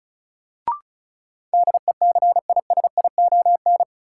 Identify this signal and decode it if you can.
Morse code, heard as a keyed tone, about 36 wpm, 700 Hz. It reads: DECISION
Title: Sentences from Top 1000 Words - New Word Drill - 35wpm